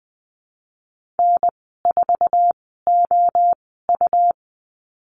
Morse code, heard as a keyed tone, about 20 wpm, spaced as for 15 wpm, 700 Hz. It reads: N4OU
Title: 207 Lesson 33g - Letter j - Call Signs 20wpm